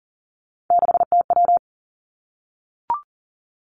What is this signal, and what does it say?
Morse code, heard as a keyed tone, about 41 wpm, 700 Hz. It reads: 6TW
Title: Three Challenging Letter-Number Combinations 40wpm